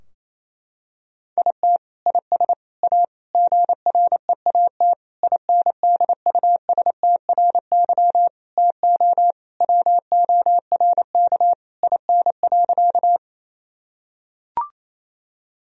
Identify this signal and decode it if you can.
Morse code, heard as a keyed tone, about 28 wpm, 700 Hz. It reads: IT IS A GREAT INDUSTRY TO WORK IN.
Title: Sentences from Top 1000 Words - Review Part 4 - 28wpm